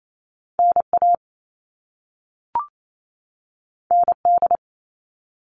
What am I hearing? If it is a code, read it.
Morse code, heard as a keyed tone, about 28 wpm, 700 Hz. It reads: NA ND
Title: All Cut Numbers - 2-Digit 28wpm